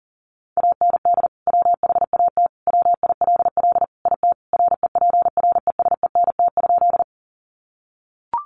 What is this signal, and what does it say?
Morse code, heard as a keyed tone, about 37 wpm, 700 Hz. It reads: AND WHAT WILL IT REPRESENT?
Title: Sentences from Top 1200 Words 40wpm